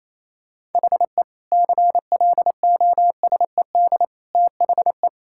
Morse code, heard as a keyed tone, about 28 wpm, 700 Hz. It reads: HE CLOSED THE
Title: Sentences from Top 1600 Words 28wpm